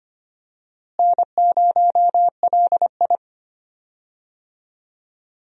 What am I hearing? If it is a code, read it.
Morse code, heard as a keyed tone, about 25 wpm, 700 Hz. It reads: N0LI
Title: USA Call Signs - 4 Characters - Repeated 3x - 25wpm